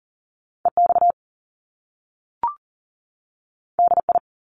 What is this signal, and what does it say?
Morse code, heard as a keyed tone, about 42 wpm, 700 Hz. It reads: EX DI
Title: Top Two Letter Combinations in Words 40wpm